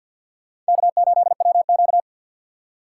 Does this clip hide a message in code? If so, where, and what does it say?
Yes: Morse code from 0.7 to 2.0 s, heard as a keyed tone, about 63 wpm, 700 Hz. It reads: K8WQ